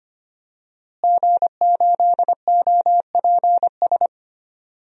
Morse code, heard as a keyed tone, about 25 wpm, 700 Hz. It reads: G8OPS
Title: DXCC-100 Call Signs - 5 Characters - Repeated 2x - 25wpm